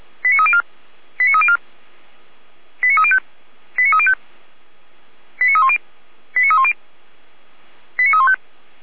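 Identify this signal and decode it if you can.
Five-tone selective call (ZVEI), heard as quick runs of five tones, tones of about 70 ms. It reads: 89304 89304 89305 89305 89310 89310 89315
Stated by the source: ZVEI five-tone call received by SDR.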